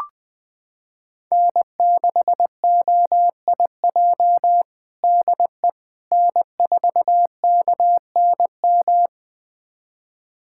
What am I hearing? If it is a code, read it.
Morse code, heard as a keyed tone, about 21 wpm, 700 Hz. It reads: N6OIJ DE N4KNM
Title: USA Callsign DE Callsign - 5 Characters - 20wpm